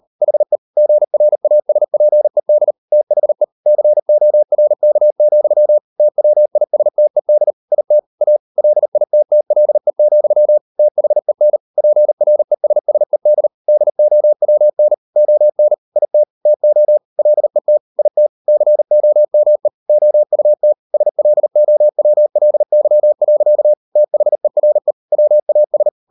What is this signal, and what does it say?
Morse code, heard as a keyed tone, about 38 wpm, 600 Hz. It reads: HE GRASPED THE CORK, TWISTED IT A LITTLE, THEN PRESSED DOWN ON IT TO LET IT COME OUT SLOWLY. THERE WAS